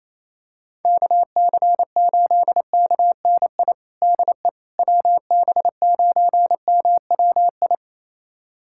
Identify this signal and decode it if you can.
Morse code, heard as a keyed tone, about 28 wpm, 700 Hz. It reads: KC8KNI DE WB9MWI
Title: USA Callsign DE Callsign 28wpm